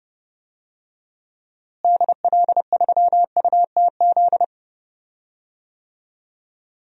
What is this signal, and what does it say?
Morse code, heard as a keyed tone, about 30 wpm, 700 Hz. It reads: DL3UTZ
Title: DXCC-100 Call Signs - 6 Characters - Repeated 3x - 30wpm